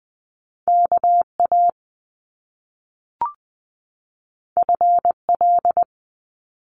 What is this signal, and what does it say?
Morse code, heard as a keyed tone, about 20 wpm, 700 Hz. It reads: K A F L
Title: Top Two Letter Combinations in Words 20wpm